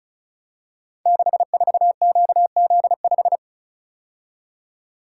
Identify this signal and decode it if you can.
Morse code, heard as a keyed tone, about 36 wpm, 700 Hz. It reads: 64QZ5
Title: Code Groups - 5 Letter:Numbers - 35wpm